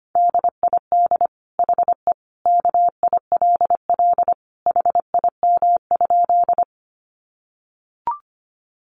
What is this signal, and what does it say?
Morse code, heard as a keyed tone, about 25 wpm, 700 Hz. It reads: DID HE KILL HIM?
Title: Sentences from Top 1400 Words 25wpm